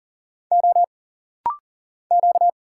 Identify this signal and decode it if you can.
Morse code, heard as a keyed tone, about 40 wpm, 700 Hz. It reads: O Q
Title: Single Letter-Number Alt - Rapid-Fire 40wpm